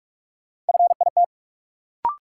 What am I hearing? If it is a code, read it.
Morse code, heard as a keyed tone, about 41 wpm, 700 Hz. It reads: FIT
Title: Sentences from Top 800 Words - New Word Drill - 45wpm